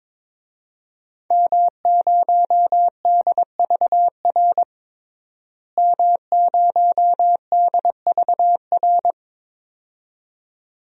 Morse code, heard as a keyed tone, about 22 wpm, 700 Hz. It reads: M0DVR M0DVR
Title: DXCC-100 Call Signs - 5 Characters - Repeated 2x - 22wpm